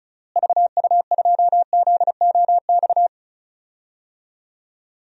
Morse code, heard as a keyed tone, about 36 wpm, 700 Hz. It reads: VU2ZOX
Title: DXCC-100 Call Signs - 6 Characters - Repeated 2x - 35wpm